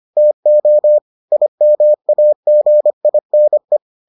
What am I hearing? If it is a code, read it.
Morse code, heard as a keyed tone, about 25 wpm, 600 Hz. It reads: TO IMAGINE